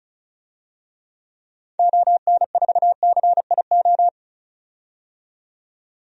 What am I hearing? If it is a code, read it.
Morse code, heard as a keyed tone, about 34 wpm, 700 Hz. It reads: ON4CIO
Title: DXCC-100 Call Signs - 6 Characters - 35wpm